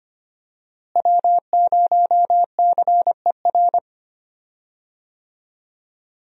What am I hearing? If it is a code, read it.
Morse code, heard as a keyed tone, about 25 wpm, 700 Hz. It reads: W0CER
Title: USA Call Signs - 5 Characters - Repeated 3x - 25wpm